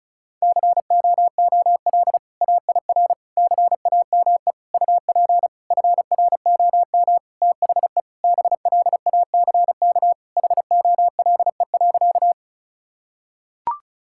Morse code, heard as a keyed tone, about 36 wpm, 700 Hz. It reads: COOL AIR CAME UP FROM THE BLACK HOLE.
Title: Sentences from Top 1600 Words 35wpm